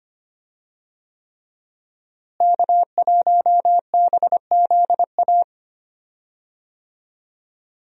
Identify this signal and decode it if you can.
Morse code, heard as a keyed tone, about 25 wpm, 700 Hz. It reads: K1BZA
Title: USA Call Signs - 5 Characters - Repeated 3x - 25wpm